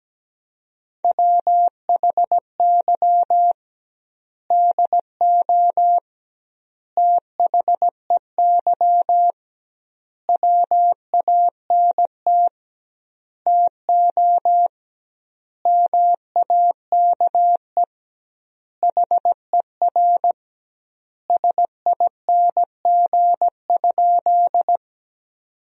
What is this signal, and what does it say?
Morse code, heard as a keyed tone, about 17 wpm, 700 Hz. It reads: WHY DO THEY WANT TO MAKE HER SING?
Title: 2x Word-Spacing - Sentences from Top 400 Words - 17wpm